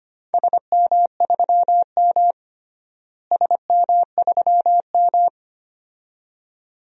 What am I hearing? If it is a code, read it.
Morse code, heard as a keyed tone, about 25 wpm, 700 Hz. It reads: SM3M SM3M
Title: DXCC-100 Call Signs - 4 Characters - Repeated 2x - 25wpm